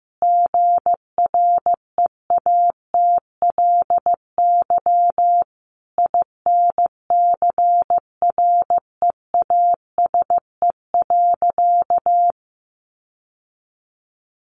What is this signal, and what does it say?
Morse code, heard as a keyed tone, about 15 wpm, 700 Hz. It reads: GREATLY INCREASE.
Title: Sentences from Top 1400 Words 15wpm